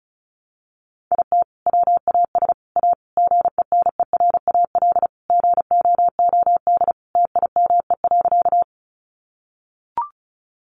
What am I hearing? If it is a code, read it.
Morse code, heard as a keyed tone, about 34 wpm, 700 Hz. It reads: IT WAS A GENERAL GOOD TIME.
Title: Sentences from Top 600 Words 35wpm